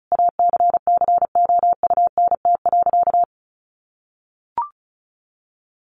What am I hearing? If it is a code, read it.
Morse code, heard as a keyed tone, about 34 wpm, 700 Hz. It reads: ACCOUNT.
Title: Sentences from Top 900 Words 35wpm